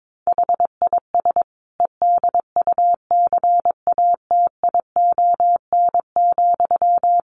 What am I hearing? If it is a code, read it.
Morse code, heard as a keyed tone, about 22 wpm, 700 Hz. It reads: HIS EDUCATION,
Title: Sentences from Top 1400 Words 22wpm